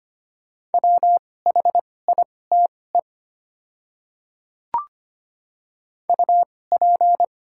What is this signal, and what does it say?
Morse code, heard as a keyed tone, about 25 wpm, spaced as for 18 wpm, 700 Hz. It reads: WHITE UP
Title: 2x Character Spacing - Top 300 Words - 25wpm